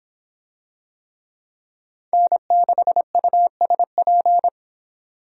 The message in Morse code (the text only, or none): N6USP